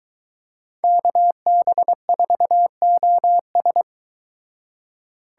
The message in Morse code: KB4OS

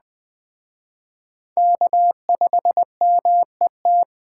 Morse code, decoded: K5MET

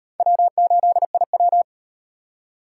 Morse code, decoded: W8IW

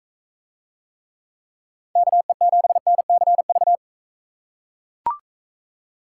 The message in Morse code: KE7NCV

